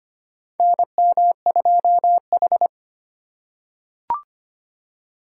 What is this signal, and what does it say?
Morse code, heard as a keyed tone, about 25 wpm, 700 Hz. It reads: NM2H
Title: USA Call Signs - 3x Character Spacing - 25wpm